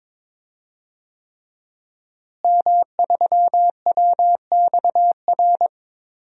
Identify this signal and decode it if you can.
Morse code, heard as a keyed tone, about 22 wpm, 700 Hz. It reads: M3WXR
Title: DXCC-100 Call Signs - 5 Characters - Repeated 3x - 22wpm